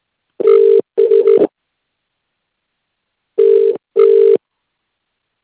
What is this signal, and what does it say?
Ringback tone (UK), 400 + 450 Hz.